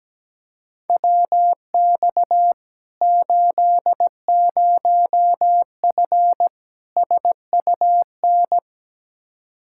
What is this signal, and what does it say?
Morse code, heard as a keyed tone, about 17 wpm, 700 Hz. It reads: WX 80F SUN
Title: Practice QSO - Exchange 2 - Info - 17wpm